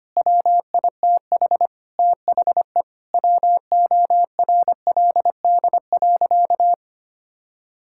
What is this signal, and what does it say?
Morse code, heard as a keyed tone, about 25 wpm, 700 Hz. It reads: WITH THE WORLD.
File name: practice / 25wpm / Difficult Sentences with Hints - 25wpm